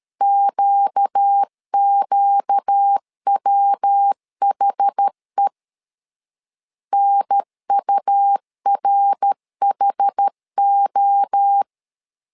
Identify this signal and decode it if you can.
Morse code, heard as a keyed tone, about 13 wpm, 800 Hz. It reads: QQWHE NURHO
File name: LEZIONI / 4 Q 12wpm 1833